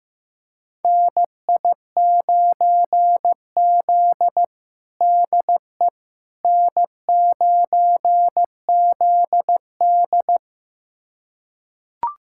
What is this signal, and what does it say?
Morse code, heard as a keyed tone, about 15 wpm, 700 Hz. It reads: NI9Z DE N9ZD
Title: USA Callsign DE Callsign - 4 Characters - 15wpm